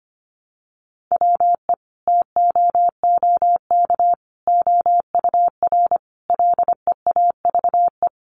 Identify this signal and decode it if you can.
Morse code, heard as a keyed tone, about 25 wpm, 700 Hz. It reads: WE TOOK OUR LEAVE
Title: Sentences from Top 400 Words 25wpm